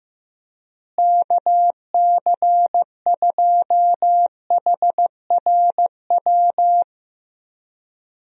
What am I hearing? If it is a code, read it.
Morse code, heard as a keyed tone, about 15 wpm, 700 Hz. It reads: KC2HRW
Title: USA Call Signs - 6 Characters - Repeated 3x - 15wpm